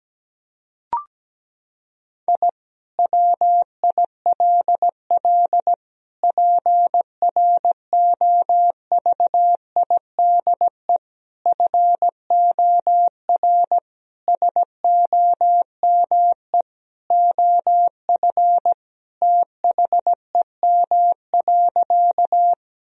Morse code, heard as a keyed tone, about 17 wpm, 700 Hz. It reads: I WILL PROVIDE FOR SOME OF THEM.
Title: Sentences from Top 1000 Words - Review Part 3 - 17wpm